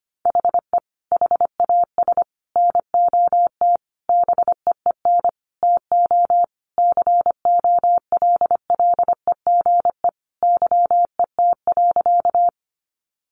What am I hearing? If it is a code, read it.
Morse code, heard as a keyed tone, about 25 wpm, 700 Hz. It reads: HE HAS NOT BEEN TO COLLEGE YET.
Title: Sentences from Top 1300 Words 25wpm